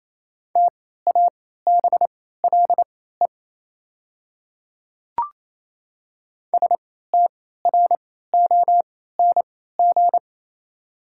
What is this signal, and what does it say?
Morse code, heard as a keyed tone, about 28 wpm, spaced as for 16 wpm, 700 Hz. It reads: TABLE STRONG
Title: 3x Character Spacing - Top 500 Words - 28wpm